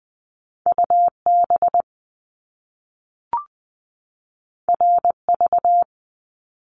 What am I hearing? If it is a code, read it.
Morse code, heard as a keyed tone, about 20 wpm, 700 Hz. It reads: UB RV
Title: Top Two Letter Combinations in Words 20wpm